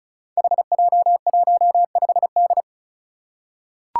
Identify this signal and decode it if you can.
Morse code, heard as a keyed tone, about 36 wpm, 700 Hz. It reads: HJ15D
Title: Five Challenging Letter-Number Combinations 35wpm